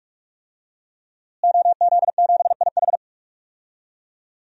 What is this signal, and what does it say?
Morse code, heard as a keyed tone, about 41 wpm, 700 Hz. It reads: OZ7IH
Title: DXCC-100 Call Signs - 5 Characters - Repeated 2x - 45wpm